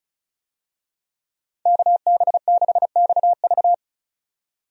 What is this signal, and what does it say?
Morse code, heard as a keyed tone, about 36 wpm, 700 Hz. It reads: KB6XV
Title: USA Call Signs - 5 Characters - 35wpm